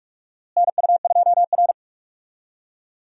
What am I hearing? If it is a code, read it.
Morse code, heard as a keyed tone, about 45 wpm, 700 Hz. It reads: NU2R
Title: USA Call Signs - 4 Characters - Repeated 3x - 45wpm